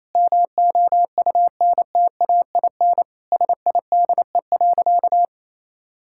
Morse code, heard as a keyed tone, about 28 wpm, 700 Hz. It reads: MOUNTAIN SIDE.
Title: Sentences from Top 1000 Words - Review Part 6 - 28wpm